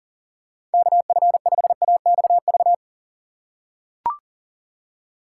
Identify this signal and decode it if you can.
Morse code, heard as a keyed tone, about 40 wpm, 700 Hz. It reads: KF5AXV
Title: USA Call Signs - 6 Characters - 4x Character Spacing - 40wpm